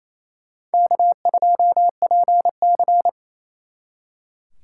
Morse code, heard as a keyed tone, about 32 wpm, 700 Hz. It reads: K2PC
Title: Speed Racing - USA 1x2 Call Signs 28wpm